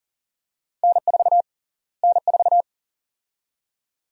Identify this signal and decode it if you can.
Morse code, heard as a keyed tone, about 40 wpm, 700 Hz. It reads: N4 N4
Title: USA Call Sign Prefixes - Repeated 2x - 40wpm